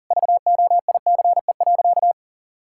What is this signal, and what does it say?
Morse code, heard as a keyed tone, about 37 wpm, 700 Hz. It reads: VOICE.
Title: Sentences from Top 1000 Words - Review Part 6 - 40wpm